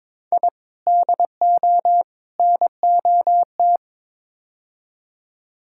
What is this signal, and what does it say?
Morse code, heard as a keyed tone, about 21 wpm, 700 Hz. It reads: I DO NOT
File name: practice / 22wpm / Build A Phrase - 5-Word Phrases - 22wpm